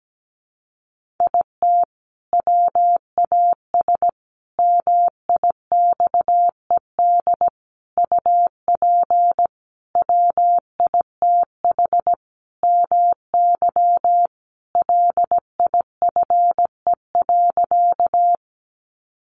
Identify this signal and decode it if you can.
Morse code, heard as a keyed tone, about 17 wpm, 700 Hz. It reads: IT WAS MIXED UP WITH MY LIFE.
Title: Sentences from Top 1500 Words 17wpm